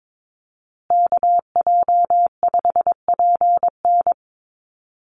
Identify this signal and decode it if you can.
Morse code, heard as a keyed tone, about 22 wpm, 700 Hz. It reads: KJ5PN
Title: USA Call Signs Repeated 2x 22wpm